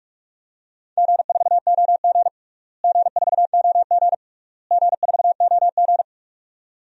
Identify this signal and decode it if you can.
Morse code, heard as a keyed tone, about 47 wpm, 700 Hz. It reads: G4OG G4OG G4OG
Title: DXCC-100 Call Signs - 4 Characters - Repeated 3x - 45wpm